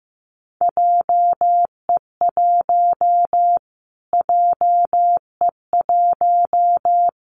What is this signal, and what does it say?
Morse code, heard as a keyed tone, about 15 wpm, 700 Hz. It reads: JE1 JE1
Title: Callsign Prefixes for Top 25 DXCC - Repeated 3x - 15wpm